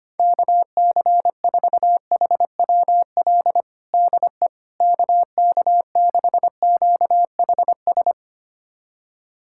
Morse code, heard as a keyed tone, about 25 wpm, 700 Hz. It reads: KC4HWL DE KK6QHS
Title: USA Callsign DE Callsign 25wpm